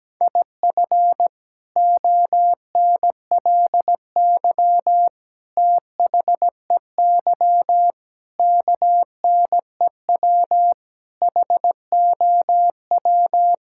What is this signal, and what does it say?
Morse code, heard as a keyed tone, about 17 wpm, 700 Hz. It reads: IF ONLY THEY KNEW HOW
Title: Sentences from Top 1000 Words - Review Part 6 - 17wpm